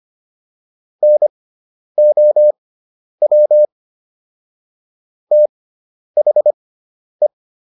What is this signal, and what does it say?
Morse code, heard as a keyed tone, about 25 wpm, spaced as for 10 wpm, 600 Hz. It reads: NOW THE